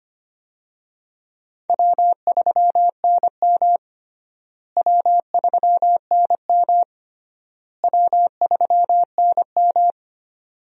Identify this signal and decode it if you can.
Morse code, heard as a keyed tone, about 25 wpm, 700 Hz. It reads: W3NM W3NM W3NM
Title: USA Call Signs - 4 Characters - Repeated 3x - 25wpm